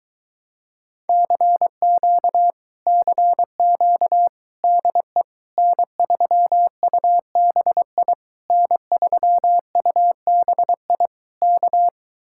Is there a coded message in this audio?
Yes: Morse code from 1.1 to 11.9 s, heard as a keyed tone, about 23 wpm, 700 Hz. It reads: CQ CQ DE N3UBI N3UBI K